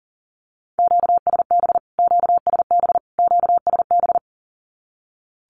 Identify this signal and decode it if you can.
Morse code, heard as a keyed tone, about 40 wpm, 700 Hz. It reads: QSB QSB QSB
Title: Q-Codes Repeated 3x 40wpm